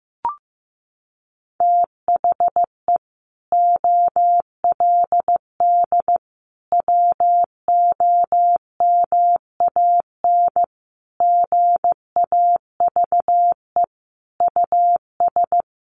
Morse code, heard as a keyed tone, about 15 wpm, 700 Hz. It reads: THE OLD WOMAN GAVE US
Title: Sentences from Top 1100 Words 15wpm